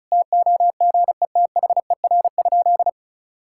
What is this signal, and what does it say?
Morse code, heard as a keyed tone, about 36 wpm, 700 Hz. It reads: TOGETHER?
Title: Sentences from Top 1600 Words 35wpm